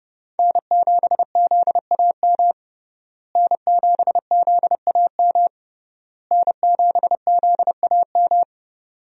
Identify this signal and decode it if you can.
Morse code, heard as a keyed tone, about 30 wpm, 700 Hz. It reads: N7ZAM N7ZAM N7ZAM
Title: USA Call Signs - 5 Characters - Repeated 3x - 30wpm